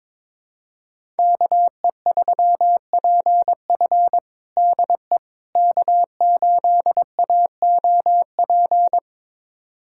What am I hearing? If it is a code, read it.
Morse code, heard as a keyed tone, about 22 wpm, 700 Hz. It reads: KE3PF DE K8AOP
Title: USA Callsign DE Callsign - 5 Characters - 22wpm